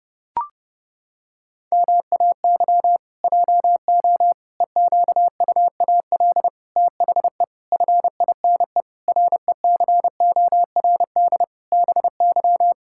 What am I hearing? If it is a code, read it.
Morse code, heard as a keyed tone, about 30 wpm, 700 Hz. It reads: MAY JO EQUAL THE FINE RECORD BY
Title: Pangram Challenge v2 - 30wpm